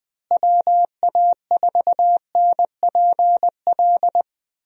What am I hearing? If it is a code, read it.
Morse code, heard as a keyed tone, about 19 wpm, 700 Hz. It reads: WA4NPL
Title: USA Call Signs - 6 Characters - Repeated 3x - 20wpm